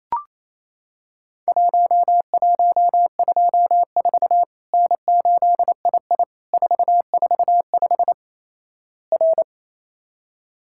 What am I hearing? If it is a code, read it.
Morse code, heard as a keyed tone, about 28 wpm, 700 Hz. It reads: 1124 N8II 445 R
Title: Worked All Europe DX Contest - QTC Exchange - 28wpm